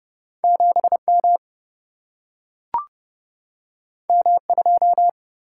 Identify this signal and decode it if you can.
Morse code, heard as a keyed tone, about 30 wpm, 700 Hz. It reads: 7M M2
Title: Single-Letter Single-Number 30wpm